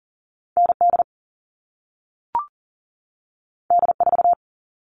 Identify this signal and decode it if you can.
Morse code, heard as a keyed tone, about 40 wpm, 700 Hz. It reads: ND D4
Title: All Cut Numbers - 2-Digit 40wpm